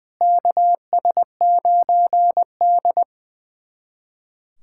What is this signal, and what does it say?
Morse code, heard as a keyed tone, about 21 wpm, 700 Hz. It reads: KS9D
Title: USA Callsign DE Callsign - 4 Characters - 20wpm